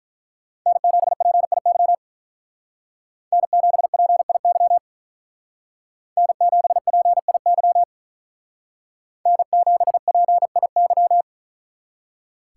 Morse code, heard as a keyed tone, about 43 wpm, 700 Hz. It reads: N7PIY N7PIY N7PIY N7PIY
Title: Speed Racing - USA 1x3 Call Signs 35wpm